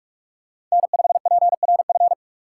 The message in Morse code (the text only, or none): N5PRF